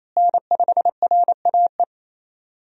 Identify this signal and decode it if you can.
Morse code, heard as a keyed tone, about 28 wpm, 700 Hz. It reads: N5RAE
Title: USA Call Signs - 5 Characters - Repeated 3x - 28wpm